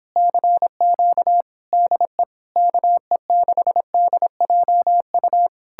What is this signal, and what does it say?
Morse code, heard as a keyed tone, about 29 wpm, 700 Hz. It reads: CQ DE KE6DJU